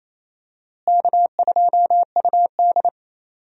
Morse code, heard as a keyed tone, about 27 wpm, 700 Hz. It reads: K2UD
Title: USA Call Signs - 4 Characters - Repeated 2x - 28wpm